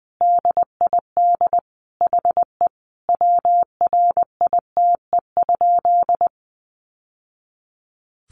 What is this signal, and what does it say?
Morse code, heard as a keyed tone, about 20 wpm, 700 Hz. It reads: DID HE WRITE?
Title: Sentences from Top 800 Words 20wpm